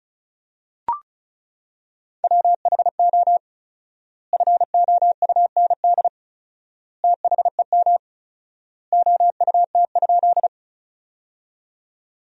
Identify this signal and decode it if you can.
Morse code, heard as a keyed tone, about 34 wpm, 700 Hz. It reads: WHO FOUND THEM OUT?
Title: 4x Word-Spacing - Sentences from Top 300 Words - 35wpm